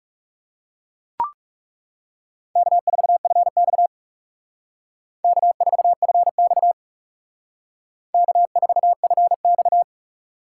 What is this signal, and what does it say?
Morse code, heard as a keyed tone, about 36 wpm, 700 Hz. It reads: K4FX K4FX K4FX
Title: Speed Racing - USA 1x2 Call Signs 30wpm